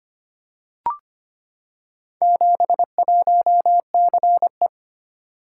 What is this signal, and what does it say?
Morse code, heard as a keyed tone, about 25 wpm, 700 Hz. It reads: 71CE
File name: practice / 25wpm / Four Letter-Number Combinations - 25wpm